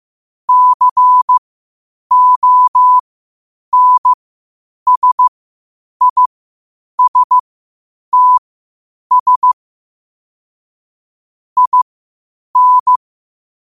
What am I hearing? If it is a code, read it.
Morse code, heard as a keyed tone, about 15 wpm, spaced as for 8 wpm, 1000 Hz. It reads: CONSISTS IN
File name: qotdfiles / 8wpm